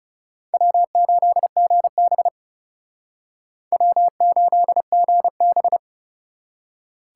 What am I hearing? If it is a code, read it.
Morse code, heard as a keyed tone, about 37 wpm, 700 Hz. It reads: W8GB W8GB
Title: Speed Racing - USA 1x2 Call Signs 30wpm